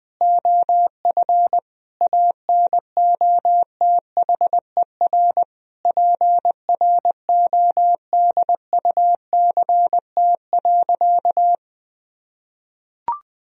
Morse code, heard as a keyed tone, about 20 wpm, 700 Hz. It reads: OF ANOTHER PRODUCT.
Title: Sentences from Top 1000 Words - Review Part 3 - 20wpm